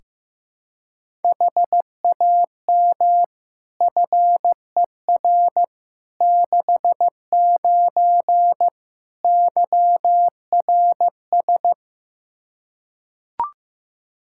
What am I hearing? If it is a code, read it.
Morse code, heard as a keyed tone, about 15 wpm, 700 Hz. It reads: HAM FER 69 YRS
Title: Practice QSO - Exchange 3 - More Info - 15wpm